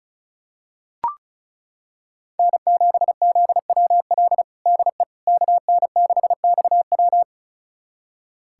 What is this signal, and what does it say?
Morse code, heard as a keyed tone, about 36 wpm, 700 Hz. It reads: N7ZWL DE KN6XW
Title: USA Callsign DE Callsign - 5 Characters - 35wpm